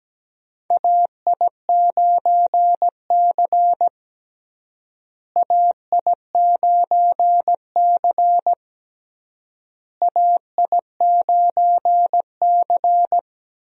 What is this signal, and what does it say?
Morse code, heard as a keyed tone, about 17 wpm, 700 Hz. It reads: AI9C AI9C AI9C
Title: USA Call Signs - 4 Characters - Repeated 3x - 17wpm